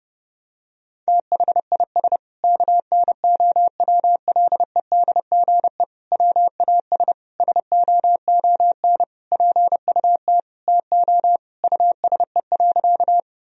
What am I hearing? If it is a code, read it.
Morse code, heard as a keyed tone, about 30 wpm, 700 Hz. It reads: THIS KNOWLEDGE WAS SOON PUT TO USE.
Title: Sentences from Top 1000 Words - Review Part 3 - 30wpm